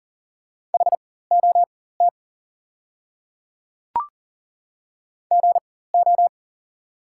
Morse code, heard as a keyed tone, about 39 wpm, spaced as for 19 wpm, 690 Hz. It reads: HOT GO
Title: 4x Character Spacing - Top 100 Words - 40wpm